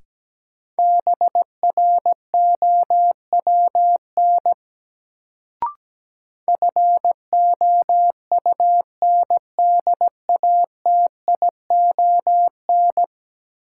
Morse code, heard as a keyed tone, about 17 wpm, 700 Hz. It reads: BROWN FOUNDATION
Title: Sentences from Top 1200 Words - New Word Drill Alt - 17wpm